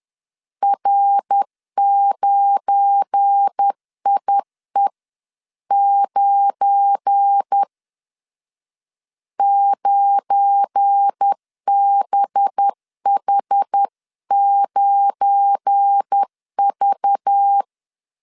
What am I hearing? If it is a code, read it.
Morse code, heard as a keyed tone, about 10 wpm, 800 Hz. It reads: R9IE9 9BH9V
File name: LEZIONI / 7 GF9Z / 7 GF9 10wpm 2747